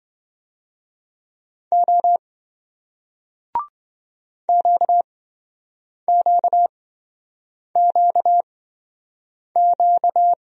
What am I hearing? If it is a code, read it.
Morse code, heard as a keyed tone, about 27 wpm, spaced as for 7 wpm, 700 Hz. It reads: O QQQQ